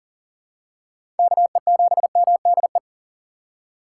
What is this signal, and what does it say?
Morse code, heard as a keyed tone, about 38 wpm, 700 Hz. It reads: KE7MDE
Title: USA Call Signs - 2x Character Spacing - 40wpm